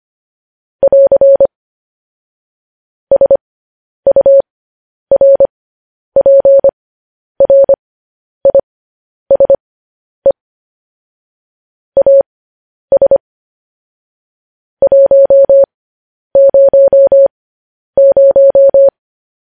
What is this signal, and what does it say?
Morse code, heard as a keyed tone, about 25 wpm, spaced as for 10 wpm, 550 Hz. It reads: <AR> SURPRISE AS 100
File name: podcast / News headlines-1x-25x10